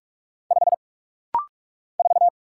Morse code, heard as a keyed tone, about 46 wpm, 700 Hz. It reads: 5 4
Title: Single Letter-Number - Rapid-Fire 45wpm